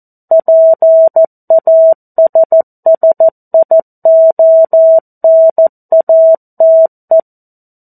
Morse code, heard as a keyed tone, about 14 wpm, 650 Hz. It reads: PASSIONATE
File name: 02-Chapter-02 14 WPM